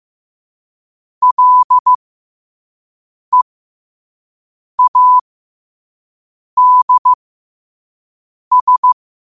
Morse code, heard as a keyed tone, about 15 wpm, spaced as for 5 wpm, 1000 Hz. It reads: LEADS